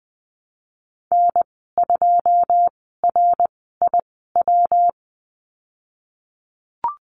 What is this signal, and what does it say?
Morse code, heard as a keyed tone, about 20 wpm, spaced as for 15 wpm, 700 Hz. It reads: N2RIW
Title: 213 Lesson 34g - Number 8 - Call Signs 20wpm